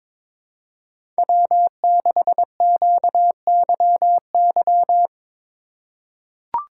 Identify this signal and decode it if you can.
Morse code, heard as a keyed tone, about 22 wpm, 700 Hz. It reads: W6QYY
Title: USA Call Signs - 5 Characters - 3x Character Spacing - 22wpm